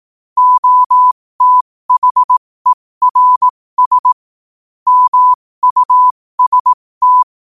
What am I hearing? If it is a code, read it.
Morse code, heard as a keyed tone, about 18 wpm, spaced as for 15 wpm, 1000 Hz. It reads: OTHERS MUST
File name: qotdfiles / 15wpm